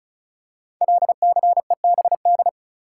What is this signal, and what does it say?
Morse code, heard as a keyed tone, about 36 wpm, 700 Hz. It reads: LCEBD
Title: Code Groups - 5 Letters - 35wpm